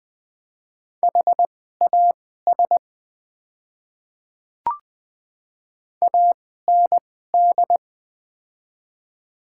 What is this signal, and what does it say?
Morse code, heard as a keyed tone, about 21 wpm, spaced as for 15 wpm, 700 Hz. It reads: HAS AND
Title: 2x Character Spacing - Top 100 Words - 20wpm